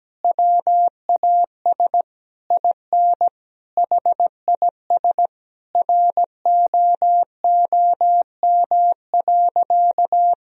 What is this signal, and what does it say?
Morse code, heard as a keyed tone, about 17 wpm, 700 Hz. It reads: WAS IN HIS ROOM.
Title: Sentences from Top 1000 Words - Review Part 4 - 17wpm